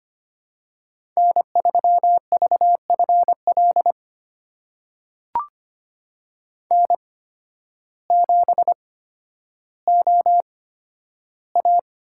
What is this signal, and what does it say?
Morse code, heard as a keyed tone, about 25 wpm, spaced as for 12 wpm, 700 Hz. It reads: N3VFL N7OA
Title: USA Call Signs - 5 Characters - 8x Character Spacing - 25wpm